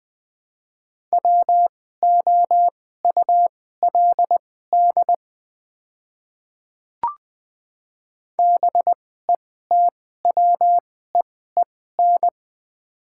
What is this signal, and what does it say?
Morse code, heard as a keyed tone, about 21 wpm, spaced as for 15 wpm, 700 Hz. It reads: WOULD BETWEEN
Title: 2x Character Spacing - Top 300 Words - 20wpm